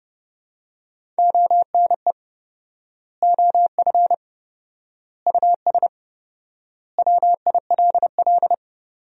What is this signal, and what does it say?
Morse code, heard as a keyed tone, about 30 wpm, 700 Hz. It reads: ONE OF US WILL
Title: 4x Word-Spacing - Sentences from Top 300 Words - 30wpm